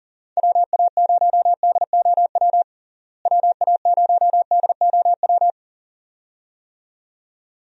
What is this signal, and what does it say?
Morse code, heard as a keyed tone, about 40 wpm, 700 Hz. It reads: WA0DOW WA0DOW
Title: USA Call Signs - 6 Characters - Repeated 2x - 40wpm